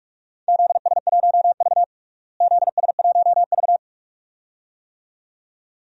Morse code, heard as a keyed tone, about 46 wpm, 700 Hz. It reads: ZS1V ZS1V
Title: DXCC-100 Call Signs - 4 Characters - Repeated 2x - 45wpm